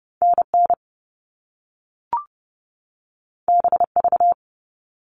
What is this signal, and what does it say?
Morse code, heard as a keyed tone, about 30 wpm, 700 Hz. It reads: NN BV
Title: All Cut Numbers - 2-Digit 30wpm